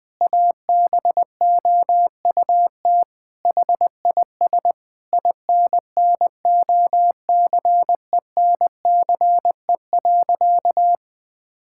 Morse code, heard as a keyed tone, about 21 wpm, 700 Hz. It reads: ABOUT HIS INNOCENCE.
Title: Sentences with 7-Word Phrases - 20wpm